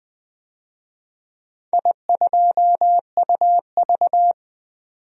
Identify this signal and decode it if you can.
Morse code, heard as a keyed tone, about 20 wpm, 700 Hz. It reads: I2UV
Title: DXCC-100 Call Signs - 4 Characters - 20wpm